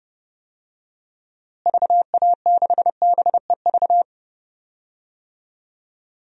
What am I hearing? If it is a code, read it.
Morse code, heard as a keyed tone, about 32 wpm, 700 Hz. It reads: VA6BEV
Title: DXCC-100 Call Signs - 6 Characters - Repeated 3x - 30wpm